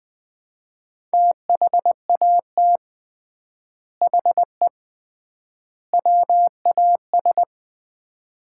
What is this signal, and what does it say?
Morse code, heard as a keyed tone, about 21 wpm, 700 Hz. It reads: THAT HE WAS
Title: 3x Word-Spacing - Sets of 3 Words - 20wpm